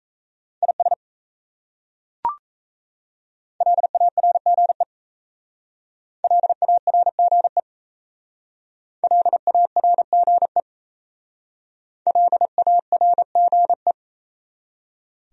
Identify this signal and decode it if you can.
Morse code, heard as a keyed tone, about 35 wpm, 700 Hz. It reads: IS LARGE LARGE LARGE LARGE